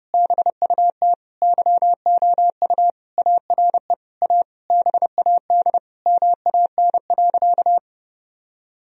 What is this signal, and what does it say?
Morse code, heard as a keyed tone, about 30 wpm, 700 Hz. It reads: BUT YOU ARE A BAD MAN.
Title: Sentences from Top 1000 Words - Review Part 1 - 30wpm